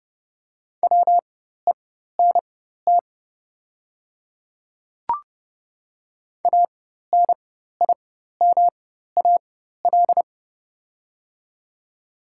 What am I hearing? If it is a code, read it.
Morse code, heard as a keyed tone, about 30 wpm, spaced as for 14 wpm, 700 Hz. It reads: WENT ANIMAL
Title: 4x Character Spacing - Top 300 Words - 30wpm